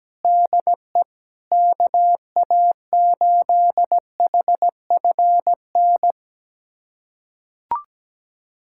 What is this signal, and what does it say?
Morse code, heard as a keyed tone, about 17 wpm, 700 Hz. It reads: DE KA8HFN
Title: USA Callsign DE Callsign - 6 Characters - 17wpm